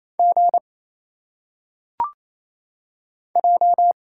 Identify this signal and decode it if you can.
Morse code, heard as a keyed tone, about 29 wpm, 700 Hz. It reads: G J
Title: Single Letters - Phonetic Pronunciation 28wpm